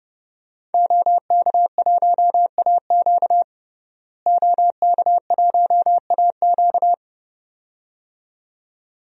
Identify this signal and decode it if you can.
Morse code, heard as a keyed tone, about 32 wpm, 700 Hz. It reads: OK1AQ OK1AQ
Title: DXCC-100 Call Signs - 5 Characters - Repeated 2x - 30wpm